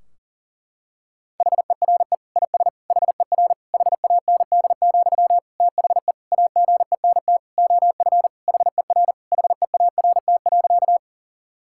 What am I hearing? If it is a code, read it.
Morse code, heard as a keyed tone, about 37 wpm, 700 Hz. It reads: HERE IS HER HAND, THE AGENT OF HER HEART.
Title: Sentences from Top 1600 Words 40wpm